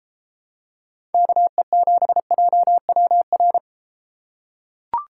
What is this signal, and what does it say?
Morse code, heard as a keyed tone, about 33 wpm, 700 Hz. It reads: KE7JWR